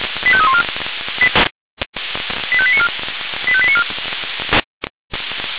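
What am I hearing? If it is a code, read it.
Five-tone selective call (ZVEI), heard as quick runs of five tones, tones of about 70 ms. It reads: 85314 85894 85894
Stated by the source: ZVEI five-tone selective call received by SDR, logged by feuerwehr-orlamuende.org fire brigade server.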